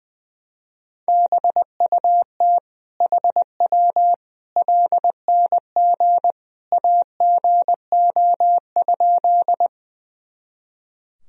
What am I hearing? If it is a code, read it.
Morse code, heard as a keyed tone, about 20 wpm, 700 Hz. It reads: BUT HW LNG AGO?
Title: Sentences from Top 500 Words and Abbreviations 20wpm